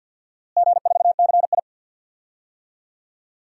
Morse code, heard as a keyed tone, about 51 wpm, 700 Hz. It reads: G4CI
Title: DXCC-100 Call Signs - 4 Characters - Repeated 2x - 50wpm